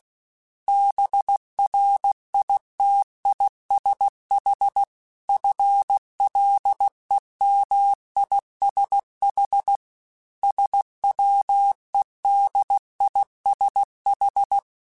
Morse code, heard as a keyed tone, about 16 wpm, 800 Hz. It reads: BRITISH FLEMISH SWEDISH
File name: General / General-mp3s / ISH @15